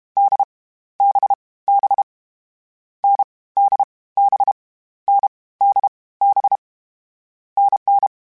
Morse code, heard as a keyed tone, about 32 wpm, 800 Hz. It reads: D B B N D B N D B NN
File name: Variable Speed / Variable Speed-mp3s / GZ GZ7 sp @30